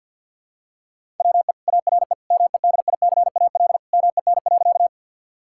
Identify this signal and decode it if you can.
Morse code, heard as a keyed tone, about 50 wpm, 700 Hz. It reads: WE ARE MEDICAL MEN.
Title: Sentences from Top 1000 Words - Review Part 5 - 50wpm